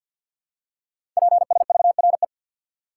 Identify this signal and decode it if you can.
Morse code, heard as a keyed tone, about 52 wpm, 700 Hz. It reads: PSVRE
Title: Code Groups - 5 Letter:Numbers - 50wpm